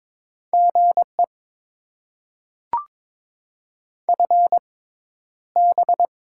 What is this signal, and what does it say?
Morse code, heard as a keyed tone, about 22 wpm, spaced as for 11 wpm, 690 Hz. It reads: GE FB
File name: Top 20 QSO Elements - 6x Character Spacing - 22wpm